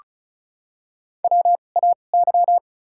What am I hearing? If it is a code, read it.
Morse code, heard as a keyed tone, about 36 wpm, spaced as for 25 wpm, 700 Hz. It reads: WAY
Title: 2x Character Spacing - Top 100 Words - 35wpm